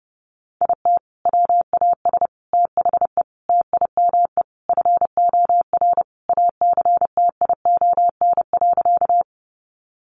Morse code, heard as a keyed tone, about 30 wpm, 700 Hz. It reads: IT WAS THE TIME FOR ACTION.
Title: Sentences from Top 700 Words 30wpm